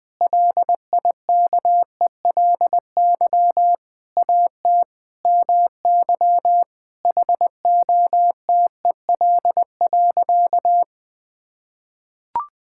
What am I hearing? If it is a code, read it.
Morse code, heard as a keyed tone, about 20 wpm, 700 Hz. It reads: LIKELY AT MY HOTEL.
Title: Sentences from Top 1600 Words 20wpm